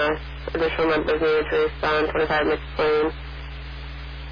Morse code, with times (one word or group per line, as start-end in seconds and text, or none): none